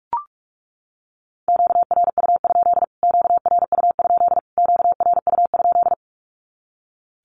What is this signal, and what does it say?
Morse code, heard as a keyed tone, about 44 wpm, 700 Hz. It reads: QRU? QRU? QRU?
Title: Q-Codes Repeated 3x 45wpm